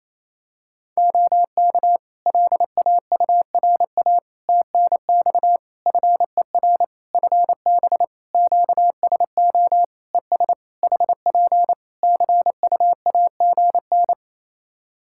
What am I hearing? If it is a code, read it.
Morse code, heard as a keyed tone, about 27 wpm, 700 Hz. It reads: OK LAURA TNX FER FB QSO ES HP CUAGN
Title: K1USN SST QSO Practice 28wpm